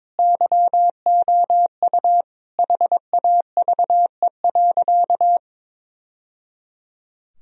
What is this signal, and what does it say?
Morse code, heard as a keyed tone, about 23 wpm, 700 Hz. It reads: YOU HAVE.
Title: Sentences from Top 1000 Words - Review Part 3 - 22wpm